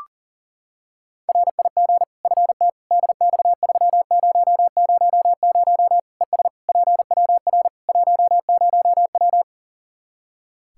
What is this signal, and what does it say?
Morse code, heard as a keyed tone, about 37 wpm, 700 Hz. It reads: RIG FT DX3000 ES PWR 10W
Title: Practice QSO - Exchange 2 - Info - 40wpm